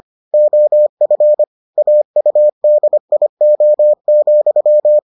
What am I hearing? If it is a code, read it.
Morse code, heard as a keyed tone, about 25 wpm, 600 Hz. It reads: OF AUDIO,